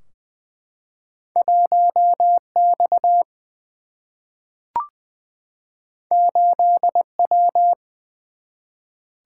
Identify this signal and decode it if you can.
Morse code, heard as a keyed tone, about 20 wpm, 700 Hz. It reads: 1 X 8 W
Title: Single-Letter Single-Number 20wpm